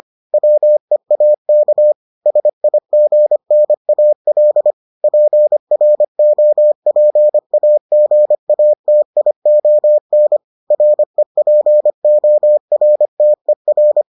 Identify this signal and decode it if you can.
Morse code, heard as a keyed tone, about 25 wpm, 600 Hz. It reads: WEAK SIGNAL PROPAGATION REPORTER